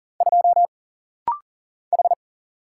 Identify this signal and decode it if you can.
Morse code, heard as a keyed tone, about 40 wpm, 700 Hz. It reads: 2 H
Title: Single Letter-Number - Rapid-Fire 40wpm